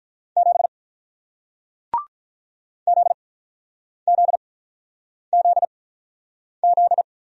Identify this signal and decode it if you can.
Morse code, heard as a keyed tone, about 48 wpm, spaced as for 9 wpm, 690 Hz. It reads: 7 ZZZZ